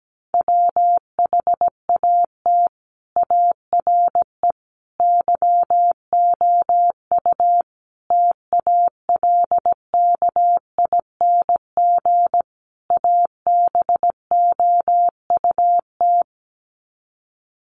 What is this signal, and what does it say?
Morse code, heard as a keyed tone, about 17 wpm, 700 Hz. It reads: WHAT ARE YOU TALKING ABOUT
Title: Sets of 5 Words 17wpm